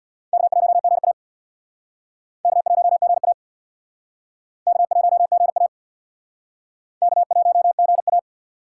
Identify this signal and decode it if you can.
Morse code, heard as a keyed tone, about 66 wpm, 700 Hz. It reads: K1GA K1GA K1GA K1GA